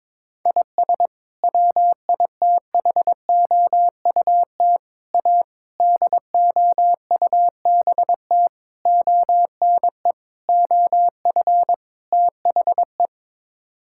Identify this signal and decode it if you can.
Morse code, heard as a keyed tone, about 22 wpm, 700 Hz. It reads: IS WITHOUT A DOUBT ONE OF THE
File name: practice / 22wpm / Sets of 7 Words - 22wpm